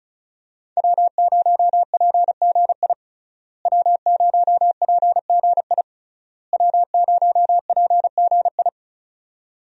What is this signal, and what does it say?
Morse code, heard as a keyed tone, about 36 wpm, 700 Hz. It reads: W0PGI W0PGI W0PGI
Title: USA Call Signs - 5 Characters - Repeated 3x - 35wpm